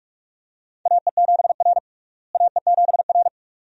Morse code, heard as a keyed tone, about 46 wpm, 700 Hz. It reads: AE7R AE7R
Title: USA Call Signs - 4 Characters - Repeated 2x - 45wpm